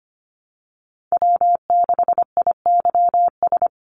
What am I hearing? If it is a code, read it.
Morse code, heard as a keyed tone, about 25 wpm, 700 Hz. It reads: W6IYS
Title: Call Signs Repeated 2x - 25wpm